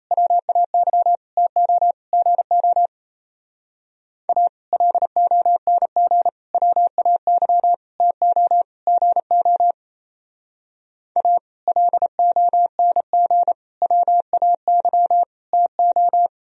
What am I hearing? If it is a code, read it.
Morse code, heard as a keyed tone, about 37 wpm, 700 Hz. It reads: WAY TO GO A LONG WAY TO GO A LONG WAY TO